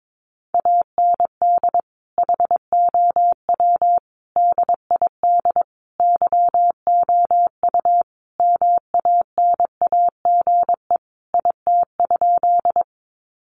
Morse code, heard as a keyed tone, about 22 wpm, 700 Hz. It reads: AND HOW DID YOU MANAGE IT?
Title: Sentences from Top 1400 Words 22wpm